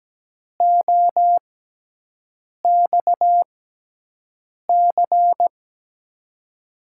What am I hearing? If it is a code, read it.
Morse code, heard as a keyed tone, about 17 wpm, spaced as for 6 wpm, 700 Hz. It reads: OXC
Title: USA Call Signs - 6 Characters - 6x Character Spacing - 17wpm